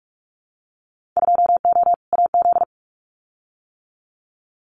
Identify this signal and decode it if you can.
Morse code, heard as a keyed tone, about 44 wpm, 700 Hz. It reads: 2O AZ
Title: Winter Field Day Exchanges 45wpm